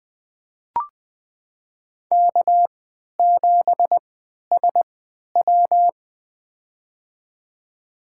Morse code, heard as a keyed tone, about 20 wpm, spaced as for 11 wpm, 700 Hz. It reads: K7SW
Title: USA Call Signs - 4 Characters - 3x Character Spacing - 20wpm